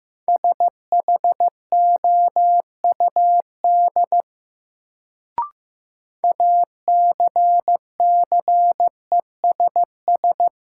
Morse code, heard as a keyed tone, about 15 wpm, 700 Hz. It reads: SHOUD ACCESS
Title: Sentences from Top 1000 Words - Review Part 4 - New Word Drill - 15wpm